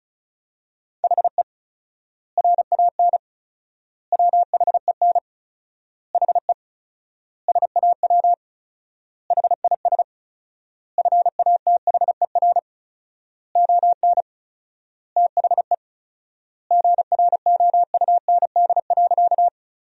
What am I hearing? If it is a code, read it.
Morse code, heard as a keyed tone, about 36 wpm, 700 Hz. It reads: HE RAN WHEN HE SAW HIS FATHER ON THE GROUND.
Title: 4x Word-Spacing - Sentences from Top 500 Words - 35wpm